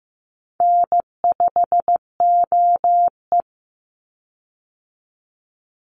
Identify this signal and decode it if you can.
Morse code, heard as a keyed tone, about 15 wpm, 700 Hz. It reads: N5OE
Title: Easy Call Signs - 15wpm